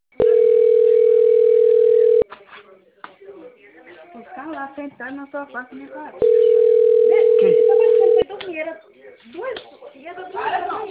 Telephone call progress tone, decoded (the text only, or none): ringback tone